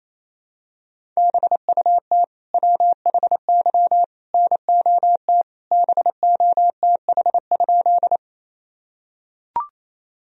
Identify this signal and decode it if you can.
Morse code, heard as a keyed tone, about 28 wpm, 700 Hz. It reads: BUT WHY NOT BOTH?
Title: Sentences from Top 1000 Words - Review Part 1 - 28wpm